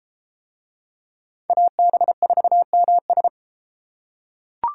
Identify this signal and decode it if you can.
Morse code, heard as a keyed tone, about 32 wpm, 700 Hz. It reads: AB4MS